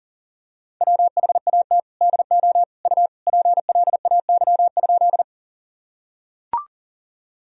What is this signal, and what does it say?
Morse code, heard as a keyed tone, about 40 wpm, 700 Hz. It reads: WHAT DO U PLAY?
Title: Sentences from Top 200 Words and Abbreviations 40wpm